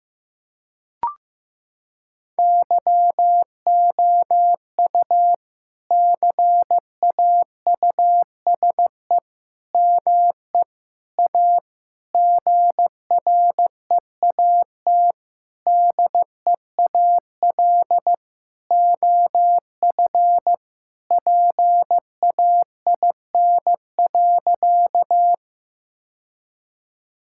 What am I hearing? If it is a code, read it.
Morse code, heard as a keyed tone, about 15 wpm, 700 Hz. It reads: YOU CAUSE ME A GREAT DEAL OF PAIN.
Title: Sentences from Top 1600 Words 15wpm